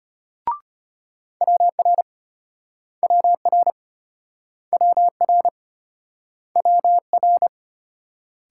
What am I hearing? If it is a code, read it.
Morse code, heard as a keyed tone, about 35 wpm, 700 Hz. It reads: WR WR WR WR